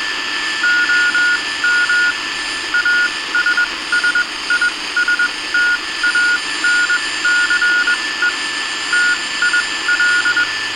Morse code, heard as a keyed tone, about 21 wpm, spaced as for 17 wpm, 1440 Hz. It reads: OM ASSISTANCE TIL